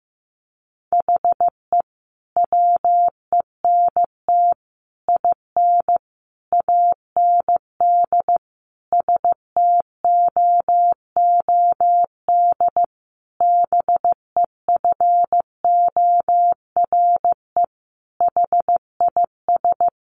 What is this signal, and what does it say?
Morse code, heard as a keyed tone, about 15 wpm, 700 Hz. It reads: HE WENT IN AND STOOD BEFORE HIS
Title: Sentences from Top 1300 Words 15wpm